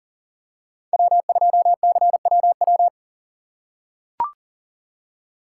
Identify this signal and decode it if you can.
Morse code, heard as a keyed tone, about 40 wpm, 700 Hz. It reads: W2CWW
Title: USA Call Signs - 5 Characters - 4x Character Spacing - 40wpm